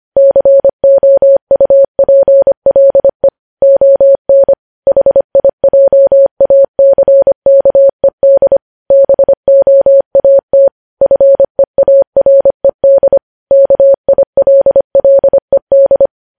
Morse code, heard as a keyed tone, about 25 wpm, 550 Hz. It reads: COUPLE ON HIJACKED BOAT FEARED KILLED
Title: News Headlines at 25